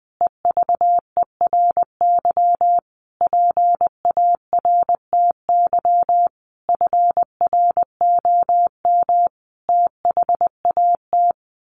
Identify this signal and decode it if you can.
Morse code, heard as a keyed tone, about 20 wpm, 700 Hz. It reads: EVERY PARTY FROM THAT
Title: Sentences from Top 1400 Words 20wpm